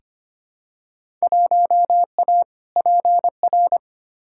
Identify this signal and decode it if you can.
Morse code, heard as a keyed tone, about 25 wpm, 700 Hz. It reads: 1A PR
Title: Field Day Exchanges Encore - 25wpm